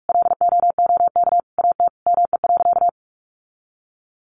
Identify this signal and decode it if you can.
Morse code, heard as a keyed tone, about 44 wpm, 700 Hz. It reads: LOOK AT ME.
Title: Sentences from Top 1500 Words 45wpm